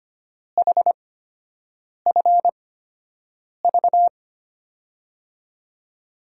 Morse code, heard as a keyed tone, about 25 wpm, spaced as for 7 wpm, 700 Hz. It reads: HFV
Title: USA Call Signs - 8x Character Spacing - 25wpm